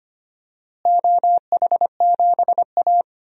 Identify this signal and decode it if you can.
Morse code, heard as a keyed tone, about 25 wpm, 700 Hz. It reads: OH7A
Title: DXCC-100 Call Signs - 4 Characters - Repeated 2x - 25wpm